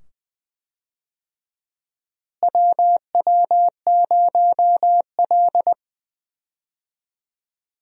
Morse code, heard as a keyed tone, about 20 wpm, 700 Hz. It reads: WW0L
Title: USA Call Signs - 4 Characters - Repeated 2x - 20wpm